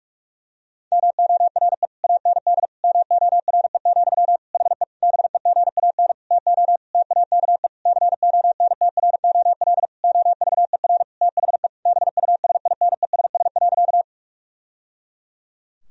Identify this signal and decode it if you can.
Morse code, heard as a keyed tone, about 46 wpm, 700 Hz. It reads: MORE AND MORE, HE BEGAN TO TAKE CONTROL OVER THE BUSINESS.
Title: Sentences from Top 1000 Words - Review Part 3 - 45wpm